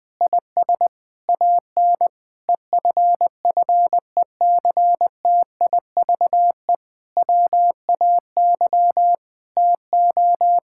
Morse code, heard as a keyed tone, about 21 wpm, 700 Hz. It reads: IS AN EFFECTIVE WAY TO
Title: Difficult Sentences with Hints - 20wpm